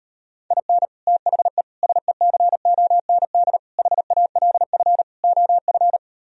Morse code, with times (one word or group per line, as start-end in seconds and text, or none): 0.5-0.9 IN
1.1-1.6 THE
1.8-3.6 SECOND
3.8-5.0 HALF
5.2-6.0 OF